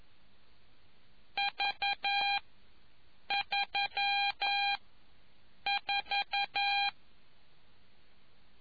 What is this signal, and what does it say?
Morse code, heard as a keyed tone, about 11 wpm, 800 Hz. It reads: V 3 4